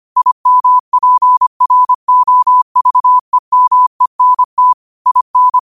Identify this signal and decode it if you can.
Morse code, heard as a keyed tone, about 25 wpm, 1000 Hz. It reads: IMPROVEMENT IN
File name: qotdfiles / 25wpm